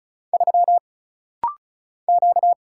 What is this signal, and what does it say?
Morse code, heard as a keyed tone, about 36 wpm, 700 Hz. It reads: 3 Q
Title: Single Letter-Number - Rapid-Fire 35wpm